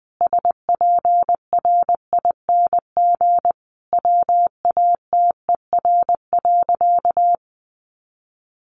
Morse code, heard as a keyed tone, about 20 wpm, 700 Hz. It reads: SPRING WATER.
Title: Sentences from Top 1200 Words 20wpm